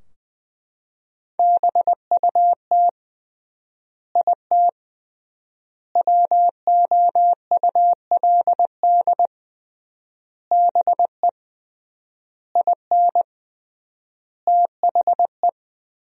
Morse code, heard as a keyed tone, about 20 wpm, 700 Hz. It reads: BUT IT WOULD BE IN THE
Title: 3x Word-Spacing - Sentences from Top 300 Words - 20wpm